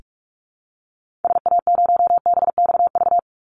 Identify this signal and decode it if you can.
Morse code, heard as a keyed tone, about 44 wpm, 700 Hz. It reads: SA0BXV
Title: Call Signs Repeated 3x - 45wpm